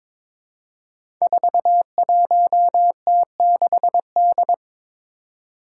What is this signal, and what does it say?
Morse code, heard as a keyed tone, about 22 wpm, 700 Hz. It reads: 41T6D
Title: Five Challenging Letter-Number Combinations 22wpm